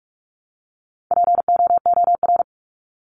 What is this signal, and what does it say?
Morse code, heard as a keyed tone, about 44 wpm, 700 Hz. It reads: POOR
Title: Top 751-1000 Words 45wpm